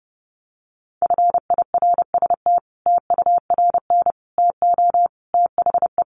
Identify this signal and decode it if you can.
Morse code, heard as a keyed tone, about 30 wpm, 700 Hz. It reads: FIRST TURN TO THE
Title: Sentences from Top 300 Words 30wpm